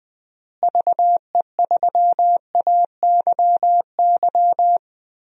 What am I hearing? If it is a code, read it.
Morse code, heard as a keyed tone, about 21 wpm, 700 Hz. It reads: VE3AYY
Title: DXCC-100 Call Signs - 6 Characters - Repeated 3x - 20wpm